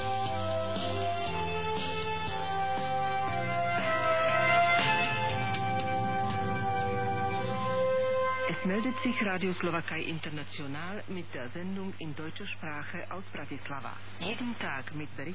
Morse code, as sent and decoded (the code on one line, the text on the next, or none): none